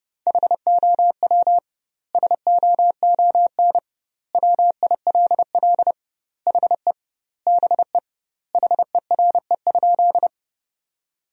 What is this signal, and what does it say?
Morse code, heard as a keyed tone, about 32 wpm, 700 Hz. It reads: HOW SOON WILL HE BE HERE?
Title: 2x Word-Spacing - Sentences from Top 400 Words - 30wpm